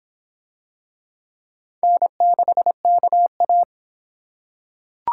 Morse code, heard as a keyed tone, about 24 wpm, 700 Hz. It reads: N6KA